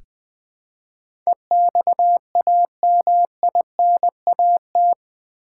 Morse code, heard as a keyed tone, about 20 wpm, 700 Hz. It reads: EXAMINAT
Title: Build A Word - 4 Letters At A Time - 11-Letter Words - 20wpm